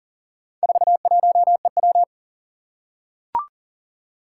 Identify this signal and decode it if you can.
Morse code, heard as a keyed tone, about 40 wpm, 700 Hz. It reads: 41EW
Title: Four Challenging Letter-Number Combinations 40wpm